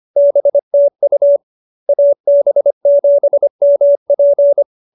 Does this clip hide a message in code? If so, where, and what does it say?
Yes: Morse code from 0.2 to 4.6 s, heard as a keyed tone, about 25 wpm, 570 Hz. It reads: BTU AB7MP